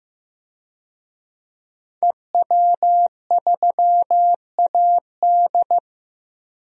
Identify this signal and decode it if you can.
Morse code, heard as a keyed tone, about 15 wpm, 700 Hz. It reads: EW3AD
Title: DXCC-100 Call Signs - 5 Characters - 15wpm